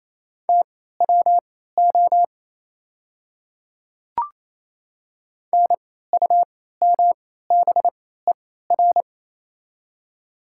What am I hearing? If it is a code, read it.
Morse code, heard as a keyed tone, about 28 wpm, spaced as for 16 wpm, 690 Hz. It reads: TWO NUMBER
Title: 3x Character Spacing - Top 100 Words - 28wpm